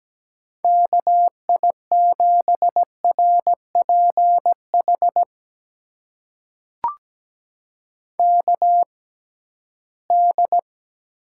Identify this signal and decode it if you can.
Morse code, heard as a keyed tone, about 17 wpm, spaced as for 13 wpm, 700 Hz. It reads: KI7RPH KD
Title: USA Call Signs - 6 Characters - 6x Character Spacing - 17wpm